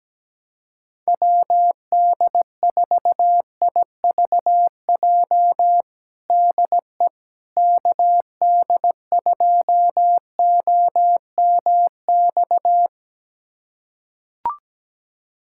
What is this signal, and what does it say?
Morse code, heard as a keyed tone, about 17 wpm, 700 Hz. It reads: WD4IVJ DE KD2OMX
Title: USA Callsign DE Callsign 17wpm